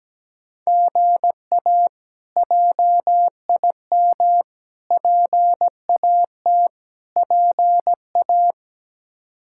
Morse code, heard as a keyed tone, about 17 wpm, 700 Hz. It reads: GA JIM PAT PA
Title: K1USN SST QSO Practice - 17wpm